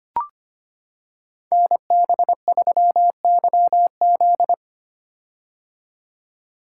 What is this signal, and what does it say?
Morse code, heard as a keyed tone, about 25 wpm, 700 Hz. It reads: NB3YZ
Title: Code Groups - 5 Letter:Numbers - 25wpm